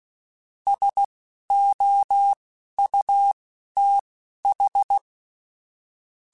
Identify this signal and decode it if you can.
Morse code, heard as a keyed tone, about 16 wpm, spaced as for 11 wpm, 800 Hz. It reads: SOUTH
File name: QSOs / QSOs-mp3s / kc7haa@10